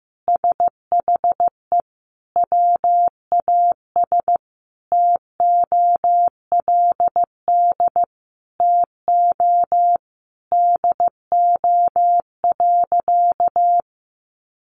Morse code, heard as a keyed tone, about 15 wpm, 700 Hz. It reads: SHE WAS TOLD TO DO.
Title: Sentences from Top 1100 Words 15wpm